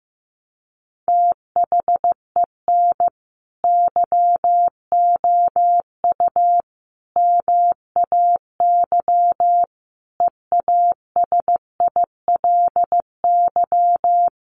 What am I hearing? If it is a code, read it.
Morse code, heard as a keyed tone, about 15 wpm, 700 Hz. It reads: THEN YOU MAY EASILY
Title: Sentences from Top 1200 Words 15wpm